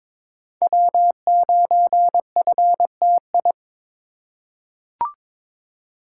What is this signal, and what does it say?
Morse code, heard as a keyed tone, about 22 wpm, 700 Hz. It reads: W9FTI
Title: USA Call Signs - 5 Characters - 2x Character Spacing - 22wpm